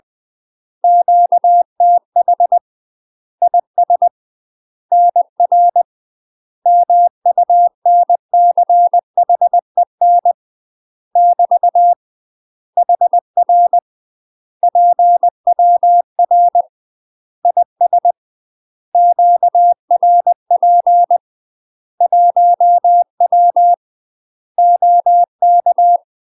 Morse code, heard as a keyed tone, about 20 wpm, 700 Hz. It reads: QTH IS NR MUNCHEN <BT> HR PWR IS QRP 1W OK
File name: UK MASTER QSOs TRAINING / 20WPM / UK MASTER QSOs 0014